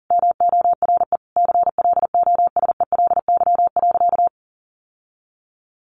Morse code, heard as a keyed tone, about 40 wpm, 700 Hz. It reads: MORE CLOSELY.
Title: Sentences from Top 1400 Words 40wpm